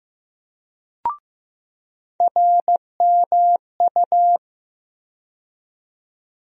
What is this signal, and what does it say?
Morse code, heard as a keyed tone, about 15 wpm, 700 Hz. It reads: RMU
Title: USA Call Sign Postfixes - 3 Letters 15wpm